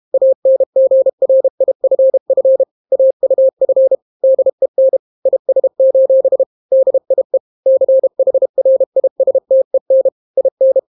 Morse code, heard as a keyed tone, about 32 wpm, 530 Hz. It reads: ANGRIFF AUF DEN IS: DIE CHRISTEN IN